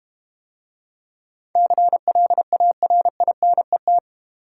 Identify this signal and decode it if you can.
Morse code, heard as a keyed tone, about 31 wpm, 700 Hz. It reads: CLARINET